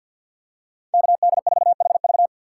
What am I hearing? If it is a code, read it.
Morse code, heard as a keyed tone, about 47 wpm, 700 Hz. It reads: KD4HV
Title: USA Call Signs - 5 Characters - 4x Character Spacing - 50wpm